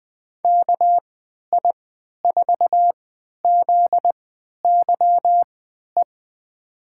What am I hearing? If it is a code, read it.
Morse code, heard as a keyed tone, about 21 wpm, spaced as for 12 wpm, 700 Hz. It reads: KI4ZYE
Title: USA Call Signs - 3x Character Spacing - 20wpm